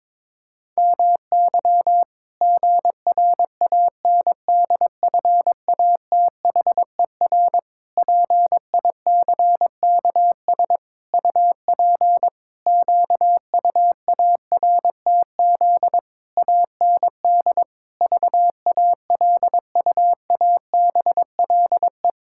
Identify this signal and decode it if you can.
Morse code, heard as a keyed tone, about 21 wpm, 700 Hz. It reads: MY GRANDFATHER PICKS UP QUARTZ AND VALUABLE
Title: Pangram Challenge v2 - 22wpm